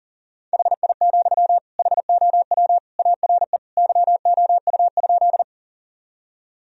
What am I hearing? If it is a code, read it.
Morse code, heard as a keyed tone, about 40 wpm, 700 Hz. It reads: HI, HOW ARE YOU?
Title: Sentences from Top 1000 Words - Review Part 3 - 40wpm